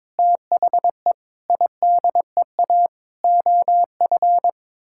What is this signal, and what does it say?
Morse code, heard as a keyed tone, about 22 wpm, 700 Hz. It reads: THE IDEA OF
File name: practice / 22wpm / Sets of 6 Words - 22wpm